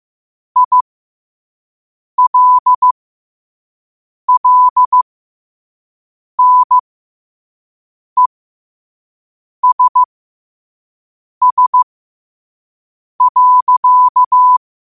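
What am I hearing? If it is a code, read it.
Morse code, heard as a keyed tone, about 15 wpm, spaced as for 5 wpm, 1000 Hz. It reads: ILLNESS.
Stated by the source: Morse code at 5 wpm.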